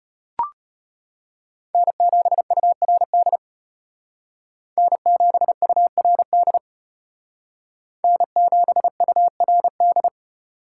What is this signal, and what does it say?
Morse code, heard as a keyed tone, about 32 wpm, 700 Hz. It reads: N7URD N7URD N7URD